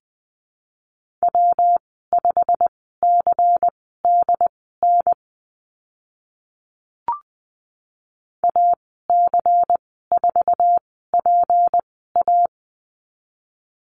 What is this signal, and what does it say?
Morse code, heard as a keyed tone, about 20 wpm, spaced as for 15 wpm, 700 Hz. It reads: W5CDN AC4PA
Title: 154 Lesson 25g - Letter p - Call Signs 20wpm